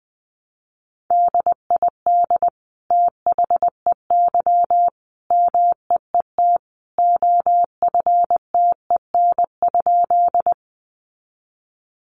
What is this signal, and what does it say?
Morse code, heard as a keyed tone, about 20 wpm, 700 Hz. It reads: DID THEY MEET OFTEN?
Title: Sentences from Top 1100 Words 20wpm